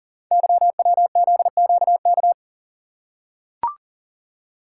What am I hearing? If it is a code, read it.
Morse code, heard as a keyed tone, about 39 wpm, 690 Hz. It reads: YWZQK
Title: Code Groups - 5 Letters - 40wpm